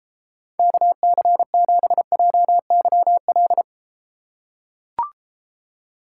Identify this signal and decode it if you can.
Morse code, heard as a keyed tone, about 34 wpm, 700 Hz. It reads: KC7JYL